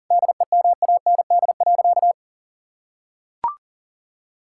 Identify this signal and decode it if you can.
Morse code, heard as a keyed tone, about 40 wpm, 700 Hz. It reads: DEMAND.
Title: Sentences from Top 1000 Words - Review Part 5 - 40wpm